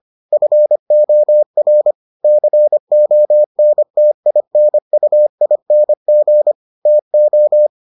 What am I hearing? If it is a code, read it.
Morse code, heard as a keyed tone, about 25 wpm, 600 Hz. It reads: FOR CONTINUING TO